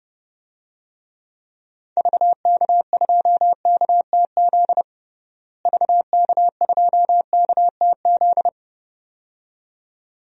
Morse code, heard as a keyed tone, about 30 wpm, 700 Hz. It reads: VK2KTZ VK2KTZ
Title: DXCC-100 Call Signs - 6 Characters - Repeated 2x - 30wpm